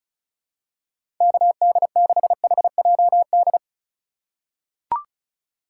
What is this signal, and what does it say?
Morse code, heard as a keyed tone, about 36 wpm, 690 Hz. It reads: KD6HJD
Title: USA Call Signs - 6 Characters - 6x Character Spacing - 35wpm